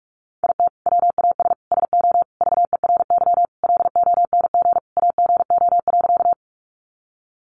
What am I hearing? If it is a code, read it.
Morse code, heard as a keyed tone, about 44 wpm, 700 Hz. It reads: IT WAS SO VERY LONG AGO.
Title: Sentences from Top 500 Words 45wpm